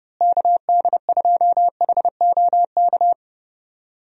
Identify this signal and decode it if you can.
Morse code, heard as a keyed tone, about 30 wpm, 700 Hz. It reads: KD2HOK
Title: USA Call Signs - 6 Characters - Repeated 2x - 30wpm